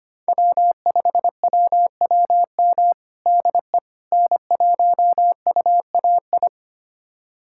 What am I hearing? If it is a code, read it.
Morse code, heard as a keyed tone, about 25 wpm, 700 Hz. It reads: W5WWM DE N1UAI
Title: USA Callsign DE Callsign - 5 Characters - 25wpm